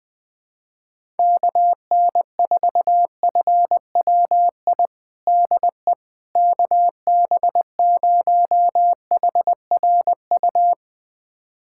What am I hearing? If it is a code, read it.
Morse code, heard as a keyed tone, about 21 wpm, 700 Hz. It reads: KN4FWI DE KB0HRU
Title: USA Callsign DE Callsign - 6 Characters - 20wpm